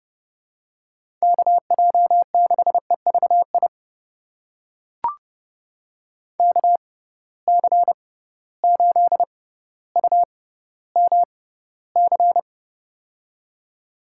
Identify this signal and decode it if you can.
Morse code, heard as a keyed tone, about 31 wpm, spaced as for 16 wpm, 700 Hz. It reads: KJ6EVI KC8UMC
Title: USA Call Signs - 6 Characters - 6x Character Spacing - 30wpm